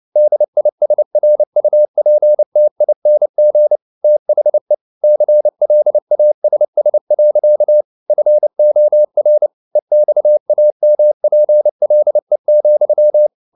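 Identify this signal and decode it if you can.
Morse code, heard as a keyed tone, about 29 wpm, 600 Hz. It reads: DISRUPTING THE CLASS. FOR EXAMPLE,
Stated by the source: Morse code at 29 wpm, sending Ken Follett's novel Never in English, from CW med Gnister.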